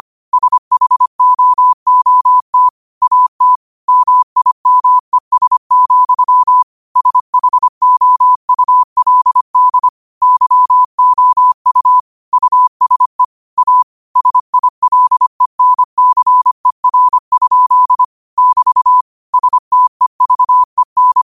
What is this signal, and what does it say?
Morse code, heard as a keyed tone, about 25 wpm, 1000 Hz. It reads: SHOOT AT MIMES, SHOULD YOU USE A SILENCER? <BT> STEVEN